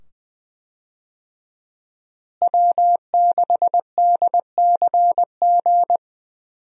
Morse code, heard as a keyed tone, about 20 wpm, 700 Hz. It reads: W6DCG
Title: USA Call Signs - 5 Characters - Repeated 2x - 20wpm